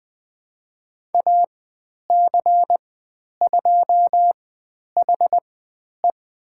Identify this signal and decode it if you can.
Morse code, heard as a keyed tone, about 20 wpm, spaced as for 10 wpm, 700 Hz. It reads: AC2HE
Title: 095 Lesson 16g - Letter c - Call Signs 20wpm